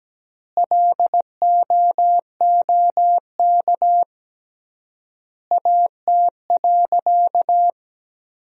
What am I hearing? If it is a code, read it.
Morse code, heard as a keyed tone, about 17 wpm, 700 Hz. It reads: LOOK AT.
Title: 3x Word-Spacing - Sentences from Top 200 Words - 17wpm